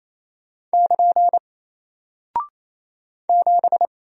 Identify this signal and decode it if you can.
Morse code, heard as a keyed tone, about 28 wpm, 700 Hz. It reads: <KN> 7
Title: Single Character (Letter:Number:Punctuation:Prosign) Alt 28wpm